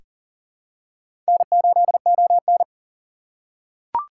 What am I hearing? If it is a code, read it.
Morse code, heard as a keyed tone, about 38 wpm, 700 Hz. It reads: N8ON
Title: USA Call Signs - 4 Characters - 3x Character Spacing - 40wpm